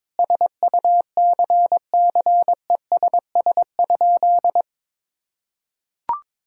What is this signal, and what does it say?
Morse code, heard as a keyed tone, about 22 wpm, 700 Hz. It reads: SUCCESS?
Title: Sentences from Top 1000 Words - Review Part 6 - 22wpm